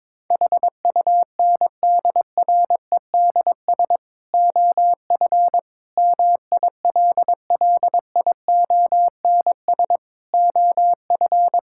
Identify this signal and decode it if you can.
Morse code, heard as a keyed tone, about 22 wpm, 700 Hz. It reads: HUNDREDS OF MILLIONS OF
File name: practice / 22wpm / Build A Phrase - 5-Word Phrases - 22wpm